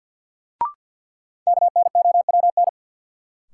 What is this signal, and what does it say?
Morse code, heard as a keyed tone, about 47 wpm, 700 Hz. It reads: KNOWN
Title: Sentences from Top 600 Words - New Word Drill - 50wpm